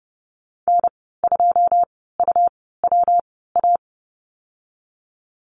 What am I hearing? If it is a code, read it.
Morse code, heard as a keyed tone, about 30 wpm, spaced as for 17 wpm, 700 Hz. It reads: N2UWA
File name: Lesson 19g - Letter w - Call Signs 30wpm